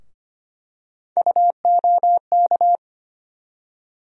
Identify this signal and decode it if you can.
Morse code, heard as a keyed tone, about 25 wpm, 700 Hz. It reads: UOK
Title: USA Call Sign Postfixes - 3 Letters 25wpm